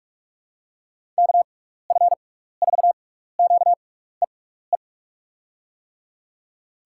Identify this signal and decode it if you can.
Morse code, heard as a keyed tone, about 43 wpm, spaced as for 15 wpm, 700 Hz. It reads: KF4QEE
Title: USA Call Signs - 6x Character Spacing - 45wpm